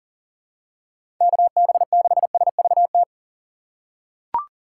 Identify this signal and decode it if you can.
Morse code, heard as a keyed tone, about 39 wpm, 700 Hz. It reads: KB6SVT
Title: USA Call Signs - 6 Characters - 6x Character Spacing - 40wpm